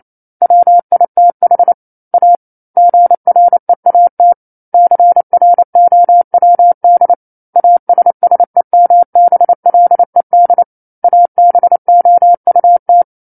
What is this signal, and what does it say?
Morse code, heard as a keyed tone, about 28 wpm, 700 Hz. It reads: WITH A GREAT CROWD ASSEMBLED ABOUT
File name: 09-Chapter-09 28 WPM